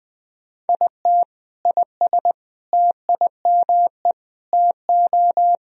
Morse code, heard as a keyed tone, about 20 wpm, 700 Hz. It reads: IT IS TIME TO
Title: Sentences from Top 1000 Words - Review Part 1 - 20wpm